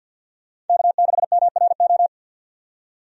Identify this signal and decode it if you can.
Morse code, heard as a keyed tone, about 45 wpm, 700 Hz. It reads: K6MRO
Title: USA Call Signs - 5 Characters - 50wpm